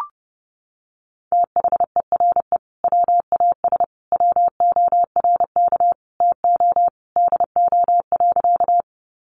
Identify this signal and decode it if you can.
Morse code, heard as a keyed tone, about 30 wpm, 700 Hz. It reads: THERE WAS WORK TO DO.
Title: Sentences from Top 200 Words 30wpm